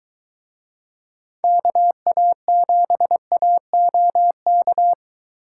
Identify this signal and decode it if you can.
Morse code, heard as a keyed tone, about 22 wpm, 700 Hz. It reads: KA7AOK